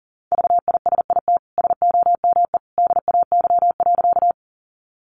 Morse code, heard as a keyed tone, about 40 wpm, 700 Hz. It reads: VISIT SOME DAY.
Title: Sentences from Top 900 Words 40wpm